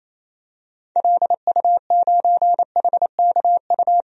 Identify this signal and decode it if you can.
Morse code, heard as a keyed tone, about 28 wpm, 700 Hz. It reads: LU9HKU
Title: DXCC-100 Call Signs - 6 Characters - Repeated 3x - 28wpm